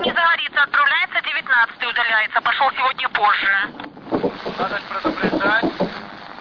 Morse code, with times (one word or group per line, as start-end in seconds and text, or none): none